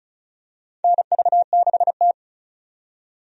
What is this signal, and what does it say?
Morse code, heard as a keyed tone, about 36 wpm, 700 Hz. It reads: NV6T
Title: USA Call Signs - 4 Characters - 35wpm